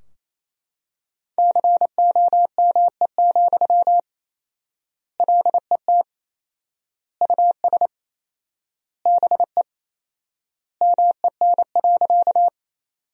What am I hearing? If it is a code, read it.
Morse code, heard as a keyed tone, about 28 wpm, 700 Hz. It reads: COME, LET US BE MEN.
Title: 4x Word-Spacing - Sentences from Top 300 Words - 28wpm